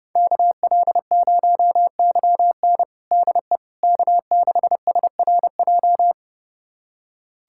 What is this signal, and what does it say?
Morse code, heard as a keyed tone, about 32 wpm, 700 Hz. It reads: KL0YN DE K6SRJ
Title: USA Callsign DE Callsign - 5 Characters - 30wpm